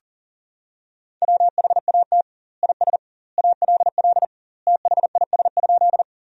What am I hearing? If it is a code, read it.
Morse code, heard as a keyed tone, about 40 wpm, 700 Hz. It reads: WHAT IS ALL THIS?
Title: 2x Word-Spacing - Sentences from Top 100 Words - 40wpm